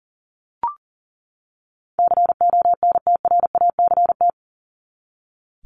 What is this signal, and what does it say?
Morse code, heard as a keyed tone, about 40 wpm, 700 Hz. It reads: CONTRACT
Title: Top 1251-1500 Words 40wpm